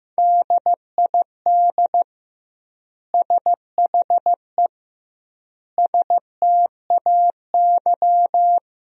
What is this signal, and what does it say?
Morse code, heard as a keyed tone, about 15 wpm, 700 Hz. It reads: DID SHE STAY
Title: 2x Word-Spacing - Sentences from Top 500 Words - 15wpm